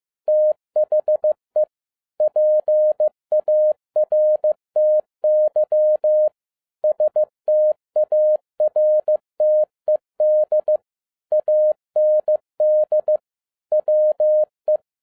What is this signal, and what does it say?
Morse code, heard as a keyed tone, about 15 wpm, 610 Hz. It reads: THE PARTY STARTED AND WE
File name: 2011-07-mp3 / n47 JOHN K3TN 15wpm